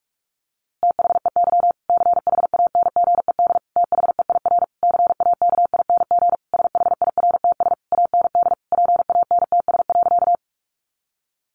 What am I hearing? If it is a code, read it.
Morse code, heard as a keyed tone, about 44 wpm, 700 Hz. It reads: THEY CHANGED THEIR CAKING SHIRTS AND PANTS.
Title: A Sound Of Thunder 45wpm